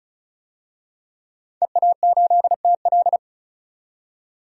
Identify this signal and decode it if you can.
Morse code, heard as a keyed tone, about 34 wpm, 700 Hz. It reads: EA8TL
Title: DXCC-100 Call Signs - 5 Characters - 35wpm